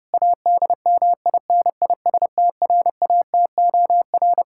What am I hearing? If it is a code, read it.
Morse code, heard as a keyed tone, about 30 wpm, 700 Hz. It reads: ADMINISTRATOR
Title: K1USN SST QSO Practice 30wpm